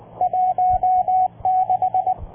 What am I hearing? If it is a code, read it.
Morse code, heard as a keyed tone, about 20 wpm, 720 Hz. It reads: 16